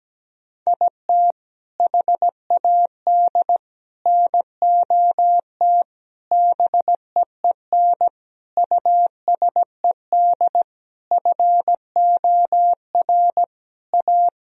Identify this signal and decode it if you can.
Morse code, heard as a keyed tone, about 17 wpm, 700 Hz. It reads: IT HAD NOT BEEN USED FOR A
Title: Sentences from Top 1000 Words - Review Part 1 - 17wpm